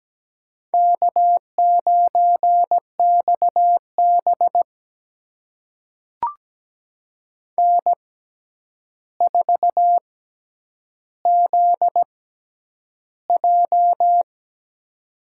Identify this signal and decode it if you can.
Morse code, heard as a keyed tone, about 17 wpm, spaced as for 9 wpm, 700 Hz. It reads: K9XB N4ZJ
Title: USA Call Signs - 4 Characters - 6x Character Spacing - 17wpm